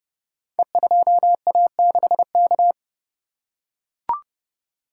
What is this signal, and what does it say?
Morse code, heard as a keyed tone, about 30 wpm, 700 Hz. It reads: E2A6K
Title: Code Groups - 5 Letter:Numbers - 30wpm